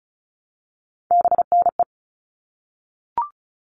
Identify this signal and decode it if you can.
Morse code, heard as a keyed tone, about 34 wpm, 700 Hz. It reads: BNE
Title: All Cut Numbers - 3-Digit 35wpm